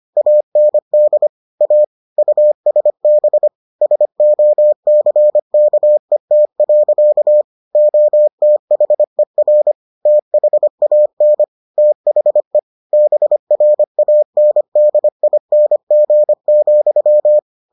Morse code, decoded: AND A USB SOCKET. OTHER THAN THE BRANDING,

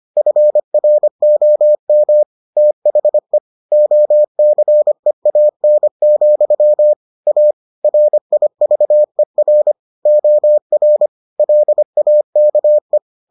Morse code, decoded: FROM THE OCEAN, A RIVER OR LAKE

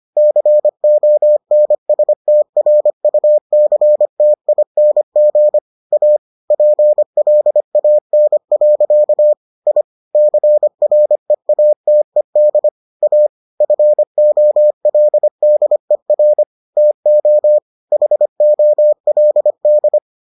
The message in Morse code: CONSTRUCTING A PLAN. I CREATED A FOLDER TO HOLD

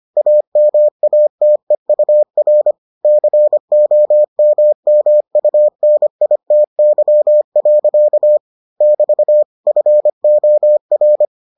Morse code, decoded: AMATEUR COMMUNITY. <BT> FOR

